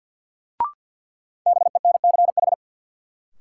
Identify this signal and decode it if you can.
Morse code, heard as a keyed tone, about 52 wpm, 700 Hz. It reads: BENCH
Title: Sentences from Top 1400 Words - New Word Drill - 50wpm